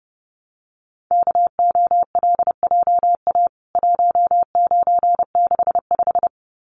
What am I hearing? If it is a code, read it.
Morse code, heard as a keyed tone, about 28 wpm, 700 Hz. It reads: KOLJA 1965
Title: CWT Exchange 30wpm